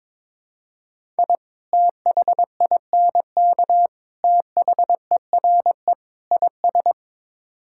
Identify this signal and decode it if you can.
Morse code, heard as a keyed tone, about 22 wpm, 700 Hz. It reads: I THINK THERE IS
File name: practice / 22wpm / Build A Phrase - 5-Word Phrases - 22wpm